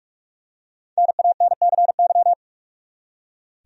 Morse code, heard as a keyed tone, about 46 wpm, 700 Hz. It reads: NANCY
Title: USA Common Names - Max 5 Letters - 45wpm